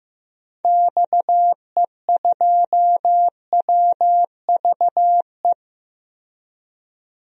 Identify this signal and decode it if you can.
Morse code, heard as a keyed tone, about 15 wpm, 700 Hz. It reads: XE2WVE
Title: DXCC-100 Call Signs - 6 Characters - Repeated 2x - 15wpm